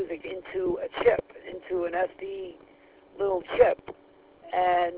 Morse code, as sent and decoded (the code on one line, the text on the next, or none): none